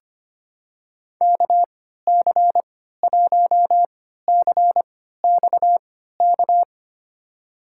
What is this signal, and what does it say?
Morse code, heard as a keyed tone, about 25 wpm, spaced as for 14 wpm, 700 Hz. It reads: KC1CXK
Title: USA Call Signs - 6 Characters - 3x Character Spacing - 25wpm